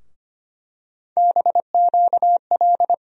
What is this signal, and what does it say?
Morse code, heard as a keyed tone, about 25 wpm, 700 Hz. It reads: BQL
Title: USA Call Sign Postfixes - 3 Letters 25wpm